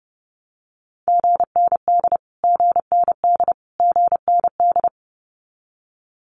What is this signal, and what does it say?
Morse code, heard as a keyed tone, about 30 wpm, 700 Hz. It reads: GND GND GND
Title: Abbreviations Repeated 3x - 30wpm